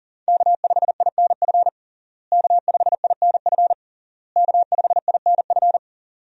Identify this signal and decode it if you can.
Morse code, heard as a keyed tone, about 40 wpm, 700 Hz. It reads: K5INF K5INF K5INF
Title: USA Call Signs - 5 Characters - Repeated 3x - 40wpm